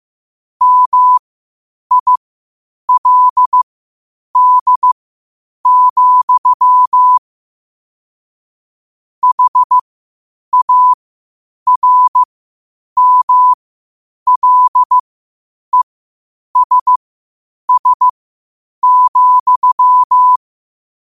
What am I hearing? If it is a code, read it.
Morse code, heard as a keyed tone, about 15 wpm, spaced as for 8 wpm, 1000 Hz. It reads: MILD, HARMLESS,
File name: qotdfiles / 8wpm